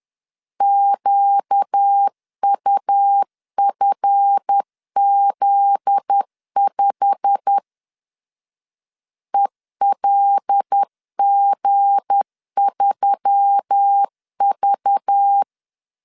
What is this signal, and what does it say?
Morse code, heard as a keyed tone, about 10 wpm, 800 Hz. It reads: QUFZ5 ELG3V
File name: LEZIONI / 10 SIGNS / 10D COMA 10wpm 3013